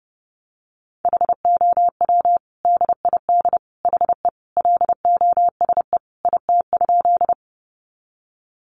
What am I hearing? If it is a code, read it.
Morse code, heard as a keyed tone, about 30 wpm, 700 Hz. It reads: HOW DID HE LOSE IT?
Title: Sentences from Top 1100 Words 30wpm